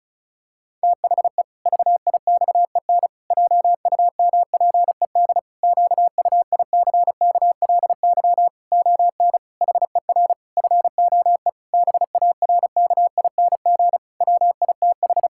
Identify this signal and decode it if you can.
Morse code, heard as a keyed tone, about 34 wpm, 700 Hz. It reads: THE VIXEN JUMPED QUICKLY ON HER FOE BARKING WITH
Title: Pangram Challenge v2 - 35wpm